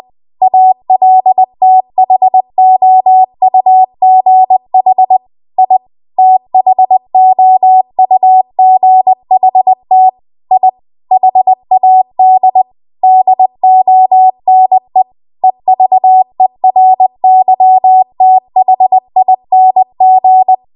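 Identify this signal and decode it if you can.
Morse code, heard as a keyed tone, about 20 wpm, 750 Hz. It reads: ALTHOUGH I THOUGHT I HAD DONE EVERYTHING
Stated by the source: Morse code practice at 20 WPM.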